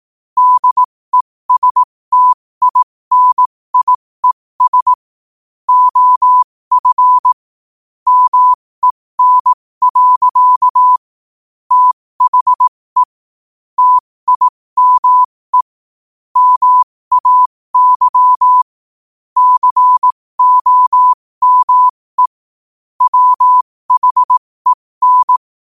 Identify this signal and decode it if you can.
Morse code, heard as a keyed tone, about 18 wpm, spaced as for 15 wpm, 1000 Hz. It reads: DESTINIES OF MEN. THE TIME MAY COME WHEN
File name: qotdfiles / 15wpm